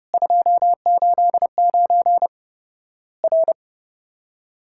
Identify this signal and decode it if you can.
Morse code, heard as a keyed tone, about 30 wpm, 700 Hz. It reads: 289 R
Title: Worked All Europe DX Contest - QTC Exchange - 30wpm